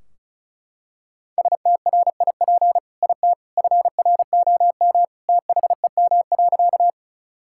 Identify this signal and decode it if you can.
Morse code, heard as a keyed tone, about 36 wpm, 700 Hz. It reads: STRIP IT FROM THEM.
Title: Sentences from Top 1600 Words 35wpm